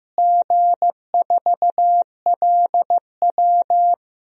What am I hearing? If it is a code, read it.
Morse code, heard as a keyed tone, about 15 wpm, 700 Hz. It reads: G4LW
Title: DXCC-100 Call Signs - 4 Characters - Repeated 2x - 15wpm